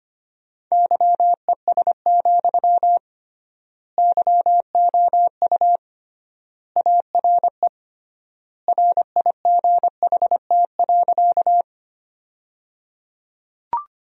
Morse code, heard as a keyed tone, about 25 wpm, 700 Hz. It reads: YES, YOU ARE RIGHT.
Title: 3x Word-Spacing - Sentences from Top 500 Words - 25wpm